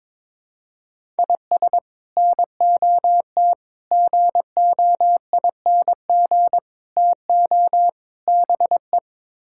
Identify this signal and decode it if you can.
Morse code, heard as a keyed tone, about 22 wpm, 700 Hz. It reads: IS NOT GOING TO BE
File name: practice / 22wpm / Build A Phrase - 5-Word Phrases - 22wpm